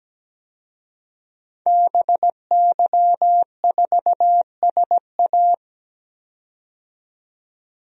Morse code, heard as a keyed tone, about 17 wpm, 700 Hz. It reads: BY4SA
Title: DXCC-100 Call Signs - 5 Characters - 17wpm